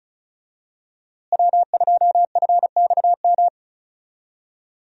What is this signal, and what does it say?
Morse code, heard as a keyed tone, about 36 wpm, 700 Hz. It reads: W2FXM
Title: USA Call Signs - 5 Characters - Repeated 2x - 35wpm